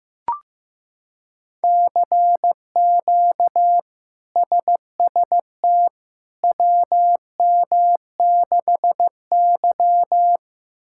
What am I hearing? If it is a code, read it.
Morse code, heard as a keyed tone, about 16 wpm, 700 Hz. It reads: CQ SST WM6Y
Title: K1USN SST QSO Practice - 15wpm